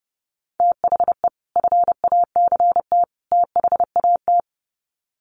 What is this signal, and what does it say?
Morse code, heard as a keyed tone, about 30 wpm, 700 Hz. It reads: THE FACT THAT
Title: Sets of 3 Words 30wpm